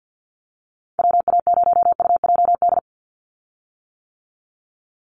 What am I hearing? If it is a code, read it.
Morse code, heard as a keyed tone, about 50 wpm, 700 Hz. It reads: WA0UJD
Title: USA Call Signs 50wpm